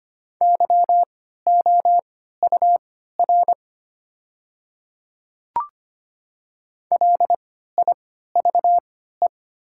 Morse code, heard as a keyed tone, about 25 wpm, spaced as for 14 wpm, 700 Hz. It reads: YOUR LIVE
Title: 3x Character Spacing - Top 300 Words - 25wpm